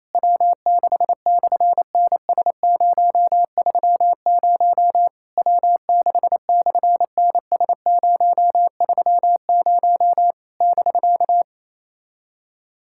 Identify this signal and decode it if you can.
Morse code, heard as a keyed tone, about 29 wpm, 700 Hz. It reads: W6/NS030 W6/NS030 <BK>
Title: SOTA QSO Practice 28wpm